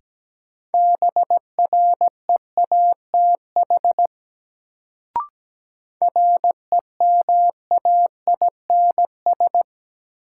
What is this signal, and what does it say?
Morse code, heard as a keyed tone, about 17 wpm, 700 Hz. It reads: BREATH REMAINS
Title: Sets of 5 Words Encore - New Word Drill - 17wpm